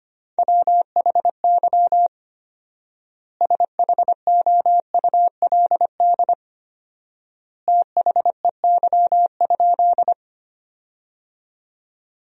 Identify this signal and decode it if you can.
Morse code, heard as a keyed tone, about 25 wpm, 700 Hz. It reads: WHY SHOULD THEY?
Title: 4x Word-Spacing - Sentences from Top 300 Words - 25wpm